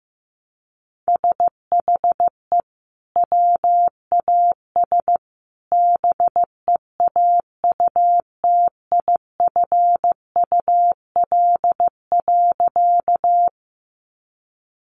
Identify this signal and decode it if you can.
Morse code, heard as a keyed tone, about 15 wpm, 700 Hz. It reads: SHE WAS BEAUTIFUL.
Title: Sentences from Top 900 Words 15wpm